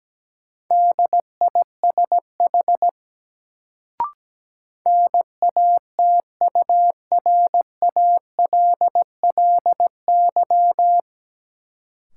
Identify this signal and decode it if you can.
Morse code, heard as a keyed tone, about 17 wpm, 700 Hz. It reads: DISH NATURALLY
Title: Sentences from Top 1300 Words - New Word Drill - 17wpm